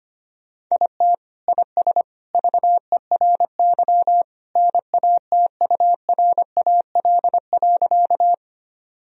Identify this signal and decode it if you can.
Morse code, heard as a keyed tone, about 25 wpm, 700 Hz. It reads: IT IS VERY NATURAL.
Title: Sentences from Top 1000 Words - Review Part 4 - 25wpm